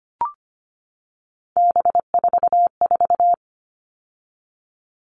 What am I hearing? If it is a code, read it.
Morse code, heard as a keyed tone, about 25 wpm, 700 Hz. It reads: B44
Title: All Cut Numbers - 3-Digit 25wpm